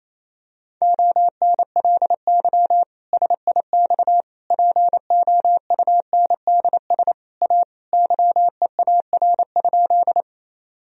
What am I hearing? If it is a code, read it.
Morse code, heard as a keyed tone, about 28 wpm, 700 Hz. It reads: ONLY SIX POUNDS A YEAR?
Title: Sentences from Top 1000 Words - Review Part 2 - 28wpm